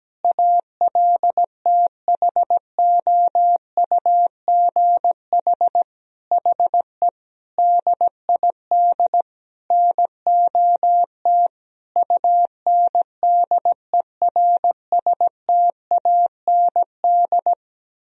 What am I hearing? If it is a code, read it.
Morse code, heard as a keyed tone, about 17 wpm, 700 Hz. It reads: ALTHOUGH HE DID NOT UNDERSTAND
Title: Sentences from Top 1000 Words - Review Part 3 - 17wpm